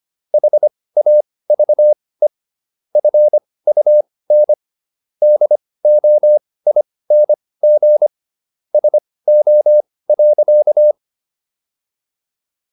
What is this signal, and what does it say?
Morse code, heard as a keyed tone, about 25 wpm, spaced as for 18 wpm, 600 Hz. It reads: HAVE FUN DOING SO.